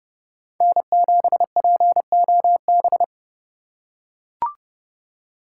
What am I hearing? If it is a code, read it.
Morse code, heard as a keyed tone, about 30 wpm, 700 Hz. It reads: N7POB
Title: USA Call Signs - 5 Characters - 2x Character Spacing - 30wpm